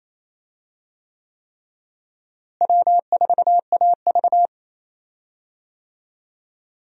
Morse code, heard as a keyed tone, about 28 wpm, 700 Hz. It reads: W4AV
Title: USA Call Signs - 4 Characters - Repeated 3x - 28wpm